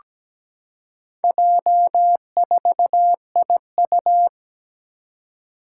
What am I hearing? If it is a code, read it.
Morse code, heard as a keyed tone, about 17 wpm, 700 Hz. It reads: J4IU
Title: Four Challenging Letter-Number Combinations 17wpm